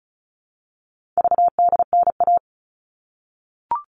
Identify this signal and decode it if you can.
Morse code, heard as a keyed tone, about 34 wpm, 700 Hz. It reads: VDNA
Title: All Cut Numbers - 4-Digit 35wpm